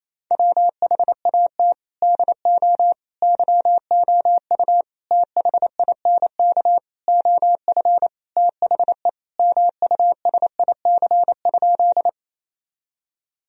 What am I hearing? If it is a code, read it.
Morse code, heard as a keyed tone, about 28 wpm, 700 Hz. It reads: WHAT DO YOU THINK OF THE MUSIC?
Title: Sentences from Top 1000 Words - Review Part 1 - 28wpm